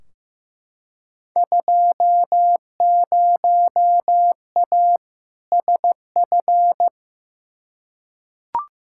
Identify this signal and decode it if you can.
Morse code, heard as a keyed tone, about 15 wpm, 700 Hz. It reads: 20A SF
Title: Field Day Exchanges Encore - 15wpm